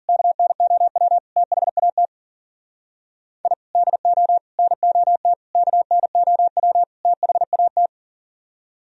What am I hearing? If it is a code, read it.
Morse code, heard as a keyed tone, about 51 wpm, 700 Hz. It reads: KNOW THAT I DO NOT KNOW THAT